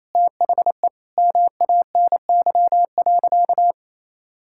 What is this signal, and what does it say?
Morse code, heard as a keyed tone, about 28 wpm, 700 Hz. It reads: THE MANY.
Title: Sentences from Top 1000 Words - Review Part 6 - 28wpm